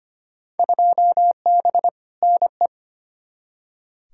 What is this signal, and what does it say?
Morse code, heard as a keyed tone, about 25 wpm, 700 Hz. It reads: 2B NE
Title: Field Day Exchanges Encore - 25wpm